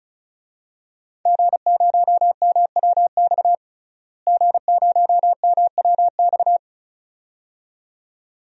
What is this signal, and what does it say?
Morse code, heard as a keyed tone, about 34 wpm, 700 Hz. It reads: G0MWX G0MWX
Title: DXCC-100 Call Signs - 5 Characters - Repeated 2x - 35wpm